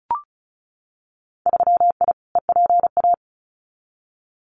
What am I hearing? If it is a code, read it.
Morse code, heard as a keyed tone, about 36 wpm, 700 Hz. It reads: 3I EPA
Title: Winter Field Day Exchanges Encore 35wpm